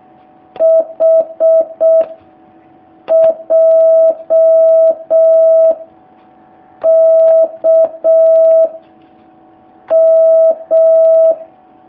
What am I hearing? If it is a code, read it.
Morse code, heard as a keyed tone, about 6 wpm, 650 Hz. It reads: H J K M
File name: jh8cbh / 2019cw